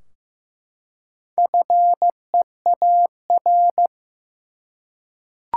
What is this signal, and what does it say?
Morse code, heard as a keyed tone, about 15 wpm, 700 Hz. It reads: FEAR
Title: Four Letter Words - 15wpm